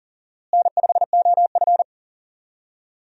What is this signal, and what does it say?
Morse code, heard as a keyed tone, about 37 wpm, 700 Hz. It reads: N5OF
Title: USA Call Signs - 4 Characters - Repeated 3x - 40wpm